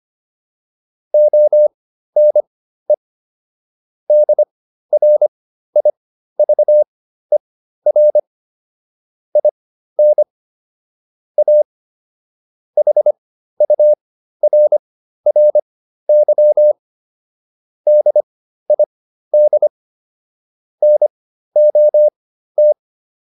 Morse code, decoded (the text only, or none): ONE DRIVER IN A HURRY DID NOT